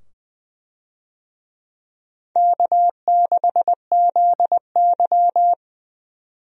Morse code, heard as a keyed tone, about 21 wpm, 700 Hz. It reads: K6ZY
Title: USA Call Signs - 4 Characters - 20wpm